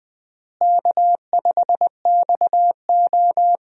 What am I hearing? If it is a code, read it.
Morse code, heard as a keyed tone, about 20 wpm, 700 Hz. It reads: K5XO
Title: USA Call Signs - 4 Characters - Repeated 3x - 20wpm